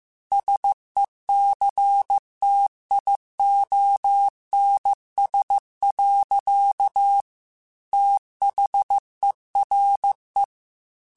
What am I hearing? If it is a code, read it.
Morse code, heard as a keyed tone, about 15 wpm, 800 Hz. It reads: SECTIONS. THERE
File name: Technical Subjects / Technical Subjects-mp3s / Sw Pwr Sup @13